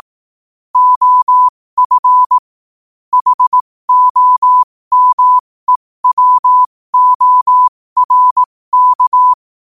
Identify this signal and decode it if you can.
Morse code, heard as a keyed tone, about 18 wpm, spaced as for 15 wpm, 1000 Hz. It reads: OF HOMEWORK
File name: qotdfiles / 15wpm